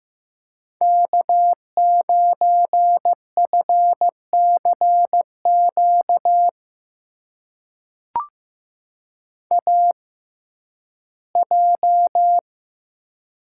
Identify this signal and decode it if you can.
Morse code, heard as a keyed tone, about 15 wpm, spaced as for 11 wpm, 700 Hz. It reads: K9FCQ AJ
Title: USA Call Signs - 5 Characters - 6x Character Spacing - 15wpm